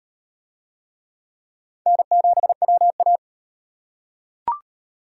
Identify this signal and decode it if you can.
Morse code, heard as a keyed tone, about 37 wpm, 700 Hz. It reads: N7WA